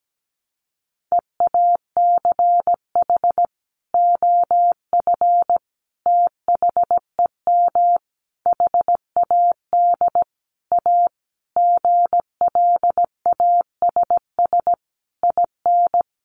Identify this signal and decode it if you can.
Morse code, heard as a keyed tone, about 17 wpm, 700 Hz. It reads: EACH OF THEM HAD A GLASS IN
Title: Sentences from Top 1300 Words 17wpm